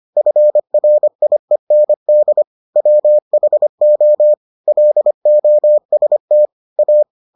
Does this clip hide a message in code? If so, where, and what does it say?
Yes: Morse code from 0.2 to 7.0 s, heard as a keyed tone, about 25 wpm, 600 Hz. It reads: FRIEND WHO LOST A